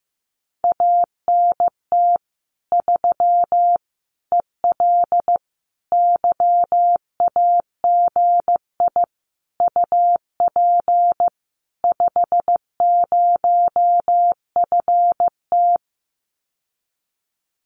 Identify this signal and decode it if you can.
Morse code, heard as a keyed tone, about 15 wpm, 700 Hz. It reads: ANT 3 EL YAGI UP 50FT
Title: Common QSO Phrases Encore 15wpm